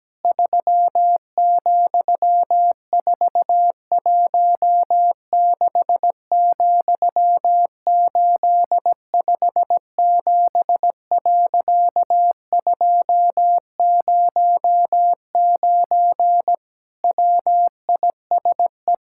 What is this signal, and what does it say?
Morse code, heard as a keyed tone, about 17 wpm, 700 Hz. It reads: 3,416,857.209 WISE
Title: Pangram Challenge v2 - 17wpm